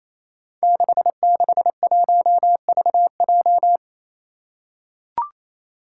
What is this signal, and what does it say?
Morse code, heard as a keyed tone, about 29 wpm, 700 Hz. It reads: 661VJ
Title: Five Challenging Letter-Number Combinations 28wpm